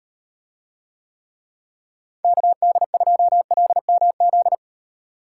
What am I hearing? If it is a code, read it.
Morse code, heard as a keyed tone, about 34 wpm, 700 Hz. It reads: KD2LMZ